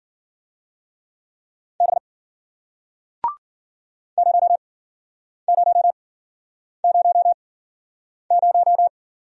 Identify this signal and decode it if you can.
Morse code, heard as a keyed tone, about 54 wpm, spaced as for 9 wpm, 700 Hz. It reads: B 0000